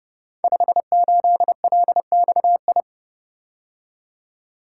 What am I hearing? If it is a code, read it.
Morse code, heard as a keyed tone, about 29 wpm, 700 Hz. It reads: 58LXI
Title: Code Groups - 5 Letter:Numbers - 30wpm